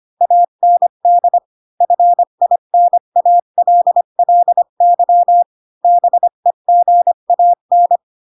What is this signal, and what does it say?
Morse code, heard as a keyed tone, about 25 wpm, 700 Hz. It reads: AND FINALLY BEGAN